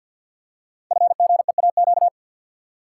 Morse code, heard as a keyed tone, about 51 wpm, 700 Hz. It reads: FGEAQ
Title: Code Groups - 5 Letters - 50wpm